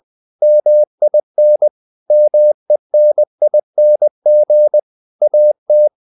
Morse code, heard as a keyed tone, about 20 wpm, 600 Hz. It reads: MIN MENING AT
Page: Ferie 20 wpm